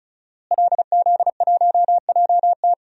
Morse code, heard as a keyed tone, about 36 wpm, 700 Hz. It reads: LZ1JT
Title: DXCC-100 Call Signs - 5 Characters - Repeated 3x - 35wpm